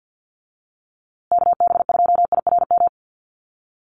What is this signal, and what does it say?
Morse code, heard as a keyed tone, about 50 wpm, 700 Hz. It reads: KB2IRM
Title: USA Call Signs Repeated 2x 50wpm